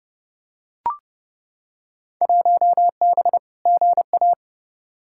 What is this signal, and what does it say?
Morse code, heard as a keyed tone, about 32 wpm, 700 Hz. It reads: 1B GA
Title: Field Day Exchanges - 30wpm